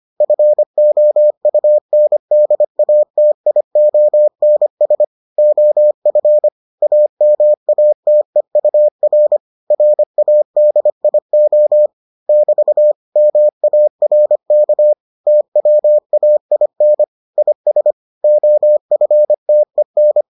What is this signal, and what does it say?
Morse code, heard as a keyed tone, about 25 wpm, 600 Hz. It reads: FOUNDATIONS OF AMATEUR RADIO <BT> MARK TWAIN IS OFTEN